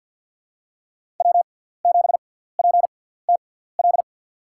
Morse code, heard as a keyed tone, about 47 wpm, spaced as for 17 wpm, 700 Hz. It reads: W7PTL
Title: USA Call Signs - 6x Character Spacing - 50wpm